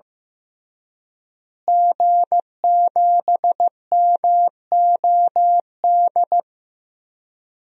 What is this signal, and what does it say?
Morse code, heard as a keyed tone, about 15 wpm, 700 Hz. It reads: G7MOD
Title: DXCC-100 Call Signs - 5 Characters - Repeated 3x - 15wpm